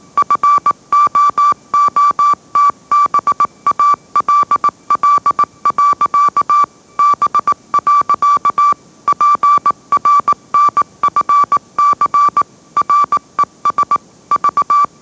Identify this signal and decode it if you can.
Morse code, heard as a keyed tone, about 18 wpm, 1160 Hz. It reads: FOOTBALL. B. PRNFC RES V
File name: mrx / mrx082 20wpm